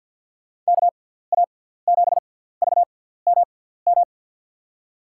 Morse code, heard as a keyed tone, about 52 wpm, spaced as for 17 wpm, 700 Hz. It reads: KA7VMM
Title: USA Call Signs - 6x Character Spacing - 50wpm